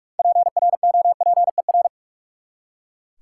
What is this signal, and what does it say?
Morse code, heard as a keyed tone, about 41 wpm, 700 Hz. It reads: PROPER
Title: Six Letter Words - 45wpm